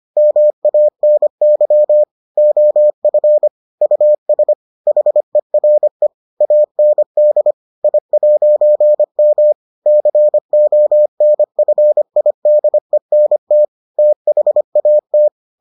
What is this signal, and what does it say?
Morse code, heard as a keyed tone, about 25 wpm, 600 Hz. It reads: MANY OF US HERE AND I'M CONFIDENT THAT